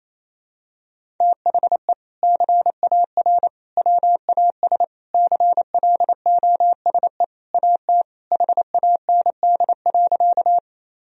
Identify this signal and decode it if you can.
Morse code, heard as a keyed tone, about 28 wpm, 700 Hz. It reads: THE CAR WAS CLOSE AT HAND.
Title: Sentences from Top 1000 Words - Review Part 1 - 28wpm